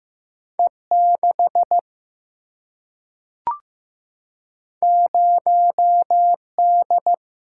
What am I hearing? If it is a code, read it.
Morse code, heard as a keyed tone, about 15 wpm, 700 Hz. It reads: E 6 0 D
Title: Single-Letter Single-Number 15wpm